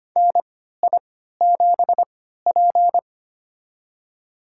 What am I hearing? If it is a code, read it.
Morse code, heard as a keyed tone, about 25 wpm, spaced as for 14 wpm, 700 Hz. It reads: NI7P
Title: USA Call Signs - 4 Characters - 3x Character Spacing - 25wpm